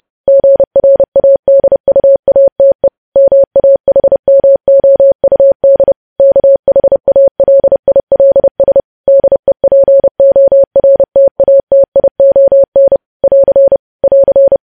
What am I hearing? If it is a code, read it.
Morse code, heard as a keyed tone, about 30 wpm, 550 Hz. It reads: GRADUATE MAHMOUD KHALILS DEPORTATION <AR> <AR>
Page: news headlines in morse code at 30 wpm